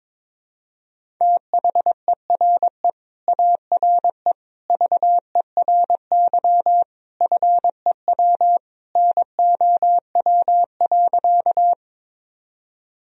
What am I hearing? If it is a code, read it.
Morse code, heard as a keyed tone, about 21 wpm, 700 Hz. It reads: THERE ARE VERY FEW NOW.
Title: Sentences from Top 1000 Words - Review Part 1 - 22wpm